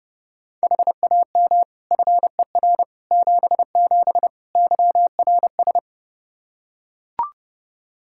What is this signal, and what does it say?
Morse code, heard as a keyed tone, about 30 wpm, 700 Hz. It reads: HAM FER 77 YRS
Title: K1USN SST QSO Practice 30wpm